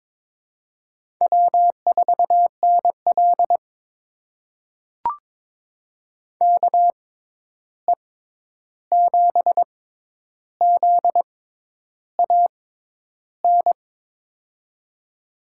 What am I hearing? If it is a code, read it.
Morse code, heard as a keyed tone, about 22 wpm, spaced as for 10 wpm, 700 Hz. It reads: W4NL KE7ZAN
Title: USA Call Signs - 6x Character Spacing - 22wpm